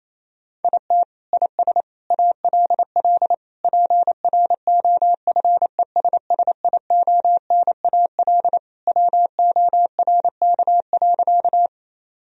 Morse code, heard as a keyed tone, about 28 wpm, 700 Hz. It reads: IT IS ALL PROFESSIONAL WORK.
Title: Sentences from Top 1000 Words - Review Part 4 - 28wpm